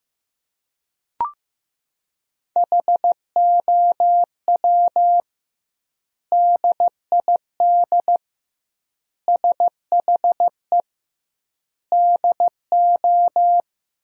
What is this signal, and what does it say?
Morse code, heard as a keyed tone, about 16 wpm, 700 Hz. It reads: HOW DID SHE DO
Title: 2x Word-Spacing - Sentences from Top 100 Words - 15wpm